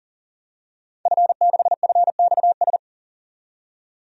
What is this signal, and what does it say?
Morse code, heard as a keyed tone, about 37 wpm, 700 Hz. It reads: F6FXS
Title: DXCC-100 Call Signs - 5 Characters - 40wpm